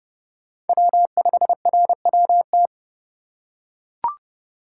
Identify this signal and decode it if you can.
Morse code, heard as a keyed tone, about 30 wpm, 700 Hz. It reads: W5RWT
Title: USA Call Signs - 5 Characters - 4x Character Spacing - 30wpm